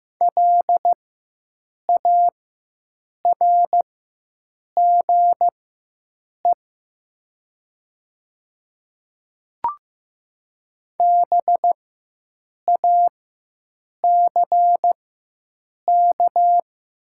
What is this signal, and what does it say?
Morse code, heard as a keyed tone, about 15 wpm, spaced as for 7 wpm, 700 Hz. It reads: LARGE BACK
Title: 4x Character Spacing - Top 300 Words - 15wpm